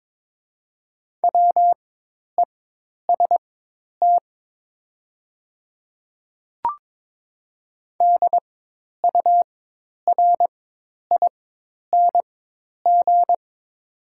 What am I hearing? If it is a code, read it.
Morse code, heard as a keyed tone, about 22 wpm, spaced as for 10 wpm, 700 Hz. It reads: WEST DURING
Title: 4x Character Spacing - Top 500 Words - 22wpm